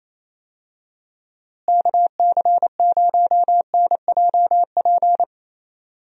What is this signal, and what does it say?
Morse code, heard as a keyed tone, about 27 wpm, 700 Hz. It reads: KC0NJP
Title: USA Call Signs - 6 Characters - Repeated 2x - 28wpm